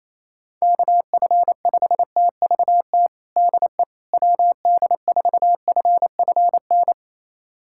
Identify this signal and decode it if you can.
Morse code, heard as a keyed tone, about 28 wpm, 700 Hz. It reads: KF5TVT DE WD4FFN
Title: USA Callsign DE Callsign - 6 Characters - 28wpm